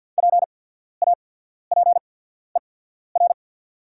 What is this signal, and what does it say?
Morse code, heard as a keyed tone, about 48 wpm, spaced as for 14 wpm, 690 Hz. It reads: PAPER
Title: 8x Character Spacing - Top 300 Words - 50wpm